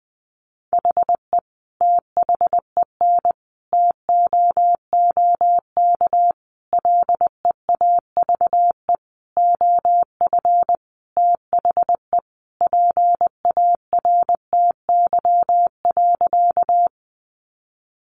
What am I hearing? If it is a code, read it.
Morse code, heard as a keyed tone, about 20 wpm, 700 Hz. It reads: HE THEN TOOK LEAVE OF THE PARTY.
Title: Sentences from Top 1400 Words 20wpm